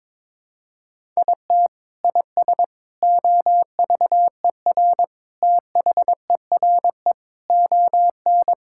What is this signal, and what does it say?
Morse code, heard as a keyed tone, about 22 wpm, 700 Hz. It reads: IT IS OVER THERE ON
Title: Sentences from Top 1000 Words - Review Part 6 - 22wpm